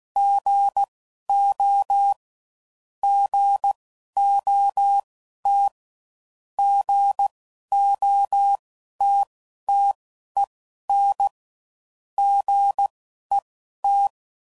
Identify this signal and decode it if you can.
Morse code, heard as a keyed tone, about 16 wpm, spaced as for 12 wpm, 800 Hz. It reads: GO GOT GOTTEN GET